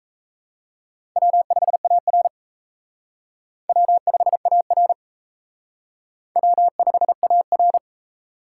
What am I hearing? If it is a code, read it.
Morse code, heard as a keyed tone, about 35 wpm, 700 Hz. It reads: W5AR W5AR W5AR
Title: Speed Racing - USA 1x2 Call Signs 28wpm